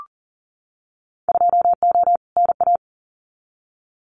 Morse code, heard as a keyed tone, about 38 wpm, 700 Hz. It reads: 2O NA
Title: Winter Field Day Exchanges 40wpm